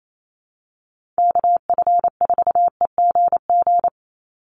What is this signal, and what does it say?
Morse code, heard as a keyed tone, about 28 wpm, 700 Hz. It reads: KF4EGG
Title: USA Call Signs 28wpm